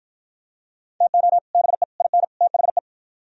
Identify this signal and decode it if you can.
Morse code, heard as a keyed tone, about 48 wpm, 700 Hz. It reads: TO BE IN THE